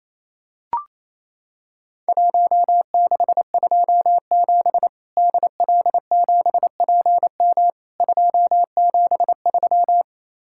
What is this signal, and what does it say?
Morse code, heard as a keyed tone, about 28 wpm, 700 Hz. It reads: 1627 DL7PM 273
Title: Worked All Europe DX Contest - QTC Exchange - 28wpm